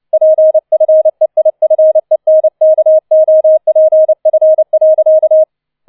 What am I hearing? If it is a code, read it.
Morse code, heard as a keyed tone, about 29 wpm, 630 Hz. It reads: PFEIFENKOPF.